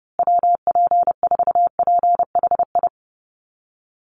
Morse code, heard as a keyed tone, about 30 wpm, 700 Hz. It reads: WP4PHI
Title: USA Call Signs Repeated 3x 30wpm